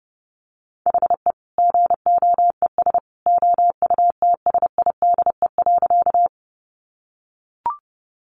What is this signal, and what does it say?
Morse code, heard as a keyed tone, about 30 wpm, 700 Hz. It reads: HE GOES OUTSIDE.
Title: Additional Easy Phrases 30wpm